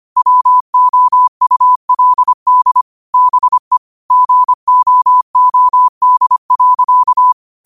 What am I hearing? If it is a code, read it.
Morse code, heard as a keyed tone, about 25 wpm, 1000 Hz. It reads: WOULD BE GOOD.